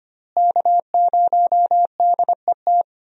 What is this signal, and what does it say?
Morse code, heard as a keyed tone, about 25 wpm, 700 Hz. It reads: K0DET
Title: USA Call Signs - 5 Characters - Repeated 3x - 25wpm